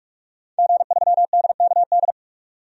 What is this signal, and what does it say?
Morse code, heard as a keyed tone, about 46 wpm, 700 Hz. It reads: G3DKD
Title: DXCC-100 Call Signs - 5 Characters - 45wpm